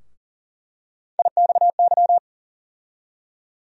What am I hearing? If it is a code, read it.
Morse code, heard as a keyed tone, about 40 wpm, 700 Hz. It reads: IXY
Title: USA Call Sign Postfixes - 3 Letters 40wpm